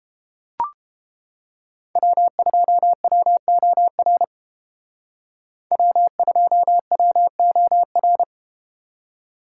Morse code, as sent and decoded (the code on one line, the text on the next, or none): .-- ..--- .-- --- .-. / .-- ..--- .-- --- .-.
W2WOR W2WOR